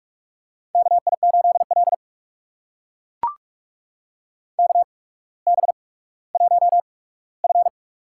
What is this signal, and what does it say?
Morse code, heard as a keyed tone, about 42 wpm, spaced as for 19 wpm, 700 Hz. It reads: KI8L KB1F
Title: USA Call Signs - 4 Characters - 8x Character Spacing - 45wpm